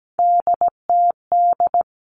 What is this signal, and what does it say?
Morse code, heard as a keyed tone, about 17 wpm, 700 Hz. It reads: DTD
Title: Three Challenging Letter-Number Combinations 17wpm